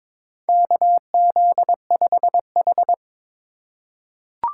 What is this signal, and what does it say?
Morse code, heard as a keyed tone, about 22 wpm, 700 Hz. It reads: KZ5H
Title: USA Call Signs - 4 Characters - 4x Character Spacing - 22wpm